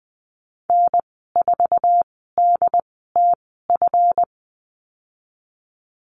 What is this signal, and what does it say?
Morse code, heard as a keyed tone, about 20 wpm, spaced as for 15 wpm, 700 Hz. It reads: N4DTF
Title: 161 Lesson 26g - Letter g - Call Signs 20wpm